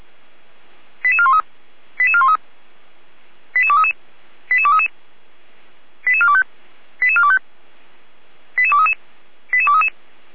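Five-tone selective call (ZVEI), heard as quick runs of five tones, tones of about 70 ms. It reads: 88413 88413 88233 88233 88425 88425 88233 88233
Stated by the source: ZVEI five-tone selective call received on SDR.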